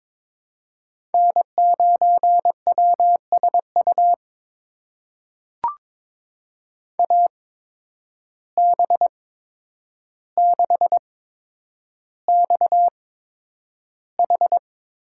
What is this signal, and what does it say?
Morse code, heard as a keyed tone, about 22 wpm, spaced as for 9 wpm, 700 Hz. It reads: N9WSU AB6XH
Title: USA Call Signs - 5 Characters - 8x Character Spacing - 22wpm